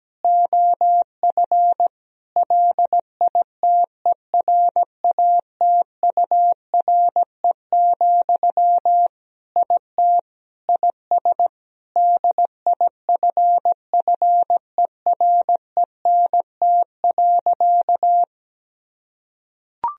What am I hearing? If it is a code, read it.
Morse code, heard as a keyed tone, about 17 wpm, 700 Hz. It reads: OF LITERATURE, IT IS DIFFERENT.
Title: Sentences from Top 1000 Words - Review Part 3 - 17wpm